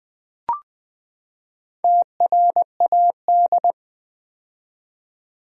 Morse code, heard as a keyed tone, about 20 wpm, 700 Hz. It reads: TRAD
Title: Build A Word - 4 Letters At A Time - 11-Letter Words - 20wpm